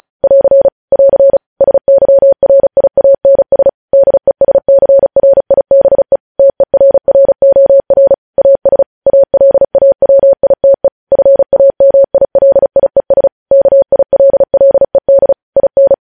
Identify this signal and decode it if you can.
Morse code, heard as a keyed tone, about 36 wpm, 550 Hz. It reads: <AR> <AR> SYRIANS DESCRIBE TERROR AS ALAWITE FAMILIES KILLED IN
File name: podcast / News headlines-1x-35x35